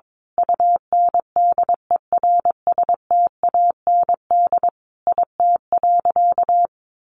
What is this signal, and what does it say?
Morse code, heard as a keyed tone, about 22 wpm, 700 Hz. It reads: UNDERSTAND IT.
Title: Sentences from Top 1400 Words 22wpm